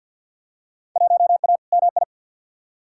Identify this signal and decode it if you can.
Morse code, heard as a keyed tone, about 52 wpm, 700 Hz. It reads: 1A MI
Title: Field Day Exchanges - 50wpm